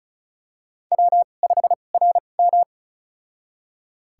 Morse code, heard as a keyed tone, about 36 wpm, spaced as for 26 wpm, 690 Hz. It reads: W5RM
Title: USA Call Signs - 4 Characters - 2x Character Spacing - 35wpm